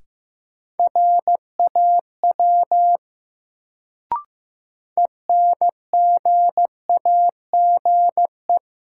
Morse code, heard as a keyed tone, about 15 wpm, 700 Hz. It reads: RAW ENGAGE
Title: Sentences from Top 1500 Words - New Word Drill Alt - 15wpm